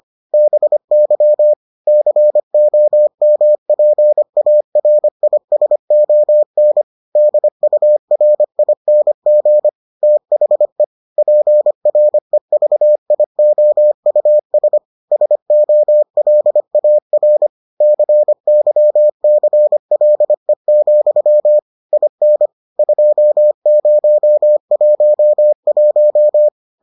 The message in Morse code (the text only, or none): BY COMPARISON DURING THE PREVIOUS SOLAR CYCLE, IN 2011